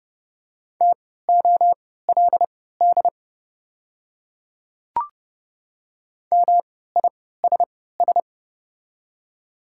Morse code, heard as a keyed tone, about 30 wpm, spaced as for 17 wpm, 690 Hz. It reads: TOLD MISS
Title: 3x Character Spacing - Top 500 Words - 30wpm